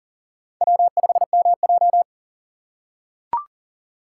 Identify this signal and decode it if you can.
Morse code, heard as a keyed tone, about 40 wpm, 700 Hz. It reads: W5MJ
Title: USA Call Signs - 4 Characters - 3x Character Spacing - 40wpm